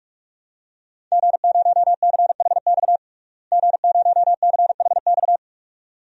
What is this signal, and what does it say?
Morse code, heard as a keyed tone, about 46 wpm, 700 Hz. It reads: G0CHX G0CHX
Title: DXCC-100 Call Signs - 5 Characters - Repeated 2x - 45wpm